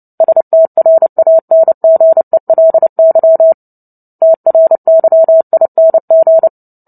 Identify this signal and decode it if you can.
Morse code, heard as a keyed tone, about 30 wpm, 650 Hz. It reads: STRANGELY TRYING
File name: Book Two / 03-Chapter 3 - 29 WPM